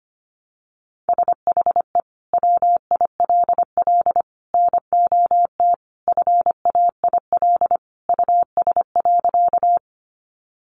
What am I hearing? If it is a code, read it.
Morse code, heard as a keyed tone, about 25 wpm, 690 Hz. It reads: SHE WILL NOT FAIL US.
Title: Sentences from Top 1100 Words 25wpm